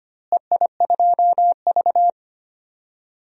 Morse code, heard as a keyed tone, about 25 wpm, 700 Hz. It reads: EI2V
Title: DXCC-100 Call Signs - 4 Characters - Repeated 3x - 25wpm